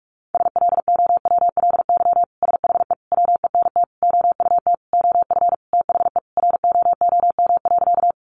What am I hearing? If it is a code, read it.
Morse code, heard as a keyed tone, about 44 wpm, 700 Hz. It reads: SLOWLY SHE WENT OUT OF THE ROOM.
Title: Sentences from Top 1100 Words 45wpm